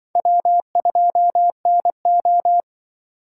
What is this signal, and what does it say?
Morse code, heard as a keyed tone, about 24 wpm, 700 Hz. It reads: W2NO
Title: Speed Racing - USA Call Signs 20wpm